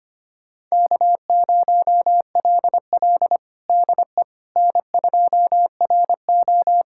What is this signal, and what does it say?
Morse code, heard as a keyed tone, about 25 wpm, 700 Hz. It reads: K0LL DE N2RO
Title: USA Callsign DE Callsign - 4 Characters - 25wpm